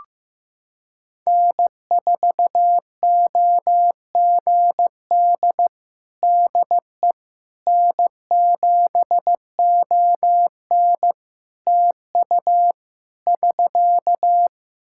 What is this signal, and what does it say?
Morse code, heard as a keyed tone, about 15 wpm, 700 Hz. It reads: N4OGD DE N7ON TU <SK>
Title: Practice QSO - Exchange 4 - Ending - 15wpm